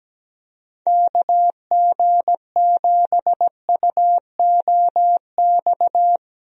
Morse code, heard as a keyed tone, about 17 wpm, 700 Hz. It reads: KG7UOX
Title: USA Call Signs - 6 Characters - Repeated 2x - 17wpm